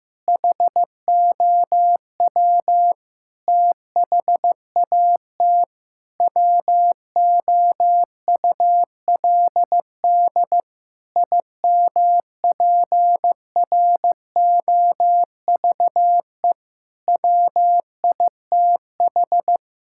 Morse code, heard as a keyed tone, about 15 wpm, 700 Hz. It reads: HOW THAT WOULD IMPROVE WITH
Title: Sentences from Top 1000 Words - Review Part 5 - 15wpm